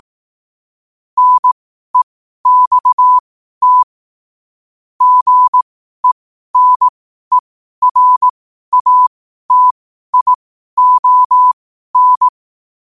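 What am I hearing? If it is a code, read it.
Morse code, heard as a keyed tone, about 18 wpm, spaced as for 12 wpm, 1000 Hz. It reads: NEXT GENERATION
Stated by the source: Morse practice sent at 12 wpm.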